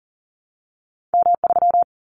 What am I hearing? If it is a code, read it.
Morse code, heard as a keyed tone, about 40 wpm, 700 Hz. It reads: M3
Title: Callsign Prefixes for Top 25 DXCC - Repeated 3x - 40wpm